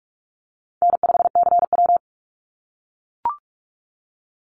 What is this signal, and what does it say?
Morse code, heard as a keyed tone, about 44 wpm, 700 Hz. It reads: N5CW
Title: Easy Call Signs - 45wpm